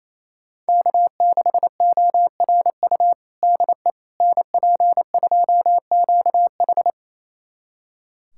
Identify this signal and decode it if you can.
Morse code, heard as a keyed tone, about 28 wpm, 700 Hz. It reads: K6ORU DE NP2QH
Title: USA Callsign DE Callsign - 5 Characters - 28wpm